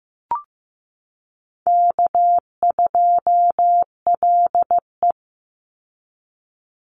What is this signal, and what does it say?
Morse code, heard as a keyed tone, about 15 wpm, 700 Hz. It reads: K2LE
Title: Easy Call Signs Repeated 3x - 15wpm